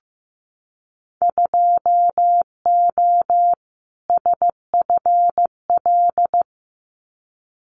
Morse code, heard as a keyed tone, about 15 wpm, 700 Hz. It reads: 2O SFL
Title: Winter Field Day Exchanges 15wpm